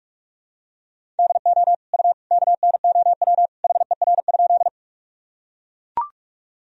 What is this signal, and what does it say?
Morse code, heard as a keyed tone, about 41 wpm, 700 Hz. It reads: DO U KNOW HER?
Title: Sentences from Top 100 Words and Abbreviations 45wpm